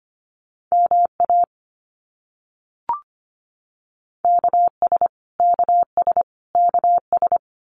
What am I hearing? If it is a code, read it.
Morse code, heard as a keyed tone, about 25 wpm, 700 Hz. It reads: MA KS KS KS
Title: US States Repeated 3x 25wpm